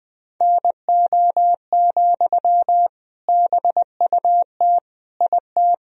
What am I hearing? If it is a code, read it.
Morse code, heard as a keyed tone, about 20 wpm, 700 Hz. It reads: NO, BUT IT
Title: Sentences from Top 1000 Words - Review Part 2 - 20wpm